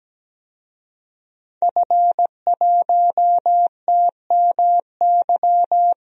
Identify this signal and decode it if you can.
Morse code, heard as a keyed tone, about 17 wpm, 700 Hz. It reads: F1TMY
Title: DXCC-100 Call Signs - 5 Characters - 17wpm